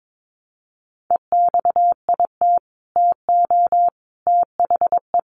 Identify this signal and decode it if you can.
Morse code, heard as a keyed tone, about 22 wpm, 700 Hz. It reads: EXIT TO THE
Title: Sentences from Top 1100 Words 22wpm